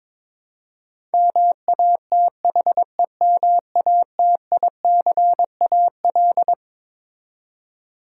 Morse code, heard as a keyed tone, about 22 wpm, 700 Hz. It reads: MATHEMATICAL
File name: Twelve Letter Words - 22wpm